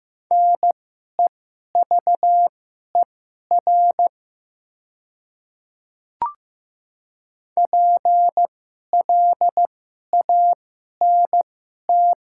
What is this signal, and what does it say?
Morse code, heard as a keyed tone, about 15 wpm, spaced as for 11 wpm, 700 Hz. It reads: NEVER PLANT
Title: 2x Character Spacing - Top 300 Words - 15wpm